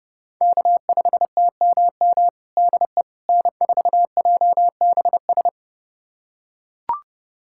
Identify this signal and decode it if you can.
Morse code, heard as a keyed tone, about 30 wpm, 700 Hz. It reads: K5TMM DE N4JBS
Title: USA Callsign DE Callsign - 5 Characters - 30wpm